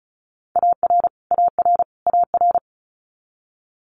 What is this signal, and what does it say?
Morse code, heard as a keyed tone, about 36 wpm, 700 Hz. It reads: AR AR AR
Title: Multi-letter Phonemes Repeated 3x 35wpm